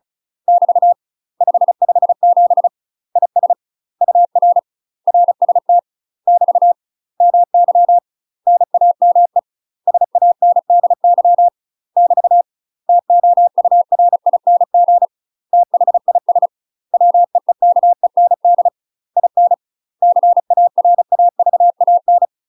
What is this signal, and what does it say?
Morse code, heard as a keyed tone, about 33 wpm, 700 Hz. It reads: <BT> 557 IS UR RST <BT> MY NAME SANDY <BT> TOURING THIS WEEKEND IN CARAVAN
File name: UK MASTER QSOs TRAINING / 35WPM / UK MASTER QSOs 0007